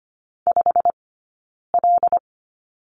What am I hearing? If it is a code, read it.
Morse code, heard as a keyed tone, about 25 wpm, spaced as for 9 wpm, 690 Hz. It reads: 5L
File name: course / 25:10wpm / 092 Lesson 16d - Letter c - Two Characters 25wpm